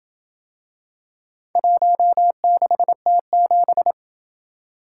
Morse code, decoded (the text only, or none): 16T7